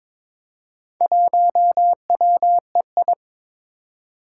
Morse code, heard as a keyed tone, about 22 wpm, 700 Hz. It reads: 1WEI
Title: Four Challenging Letter-Number Combinations 22wpm